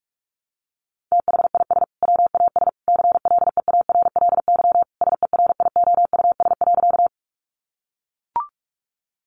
Morse code, heard as a keyed tone, about 44 wpm, 700 Hz. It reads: THIS WAS CLEARLY SERIOUS.
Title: Sentences from Top 1300 Words 45wpm